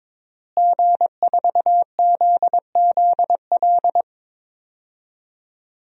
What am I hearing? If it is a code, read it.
Morse code, heard as a keyed tone, about 22 wpm, 700 Hz. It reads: G4ZZL
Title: DXCC-100 Call Signs - 5 Characters - Repeated 3x - 22wpm